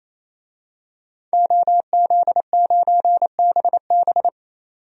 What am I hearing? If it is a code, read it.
Morse code, heard as a keyed tone, about 28 wpm, 700 Hz. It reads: OZ9BB
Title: DXCC-100 Call Signs - 5 Characters - 28wpm